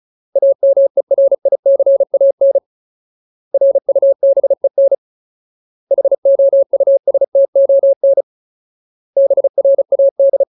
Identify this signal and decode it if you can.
Morse code, heard as a keyed tone, about 34 wpm, 550 Hz. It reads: AMERICAN RUBEN HOUSTON BRAD
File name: Ultimate Word List / 35WPM Ultimate